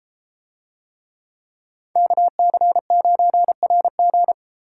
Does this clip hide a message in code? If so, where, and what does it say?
Yes: Morse code from 2.0 to 4.3 s, heard as a keyed tone, about 31 wpm, 700 Hz. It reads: KC9RG